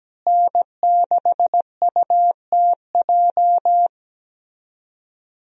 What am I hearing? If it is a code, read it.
Morse code, heard as a keyed tone, about 17 wpm, 700 Hz. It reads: N6UTJ
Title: USA Call Signs - 5 Characters - Repeated 2x - 17wpm